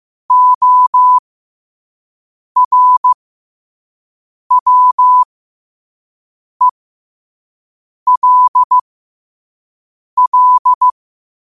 Morse code, heard as a keyed tone, about 15 wpm, spaced as for 5 wpm, 1000 Hz. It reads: ORWELL
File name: 5wpm